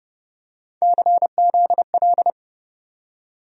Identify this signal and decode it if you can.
Morse code, heard as a keyed tone, about 30 wpm, 700 Hz. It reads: CZL
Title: USA Call Sign Postfixes - 3 Letters 30wpm